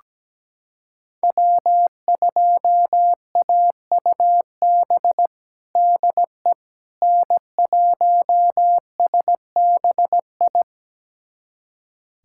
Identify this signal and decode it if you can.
Morse code, heard as a keyed tone, about 17 wpm, 700 Hz. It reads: W2AUB DE N1SBI
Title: USA Callsign DE Callsign 17wpm